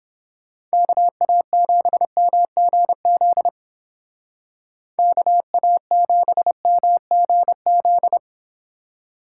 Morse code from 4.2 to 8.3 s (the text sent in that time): KA7MGZ